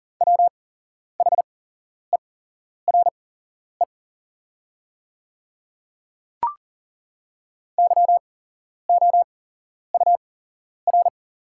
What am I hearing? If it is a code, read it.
Morse code, heard as a keyed tone, about 40 wpm, spaced as for 11 wpm, 700 Hz. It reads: WHERE YOUR
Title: 8x Character Spacing - Top 500 Words - 40wpm